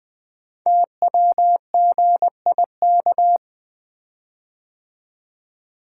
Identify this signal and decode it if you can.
Morse code, heard as a keyed tone, about 20 wpm, 700 Hz. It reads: TWGIK
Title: Code Groups - 5 Letters - 20wpm